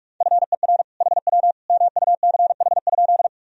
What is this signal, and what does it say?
Morse code, heard as a keyed tone, about 46 wpm, 700 Hz. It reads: FER HW MUCH?
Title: Sentences from Top 200 Words and Abbreviations 45wpm